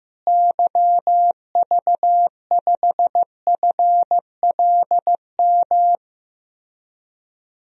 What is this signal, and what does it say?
Morse code, heard as a keyed tone, about 15 wpm, 700 Hz. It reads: YV5FLM
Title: DXCC-100 Call Signs - 6 Characters - Repeated 3x - 15wpm